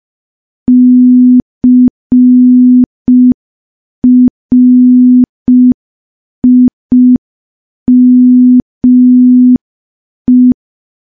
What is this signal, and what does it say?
Morse code, heard as a keyed tone, about 5 wpm, 250 Hz. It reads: CRIME
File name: morse / morse-training-05WPM